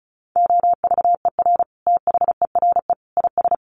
Morse code, heard as a keyed tone, about 34 wpm, 700 Hz. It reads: OVER THERE IS
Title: Sentences from Top 1200 Words 35wpm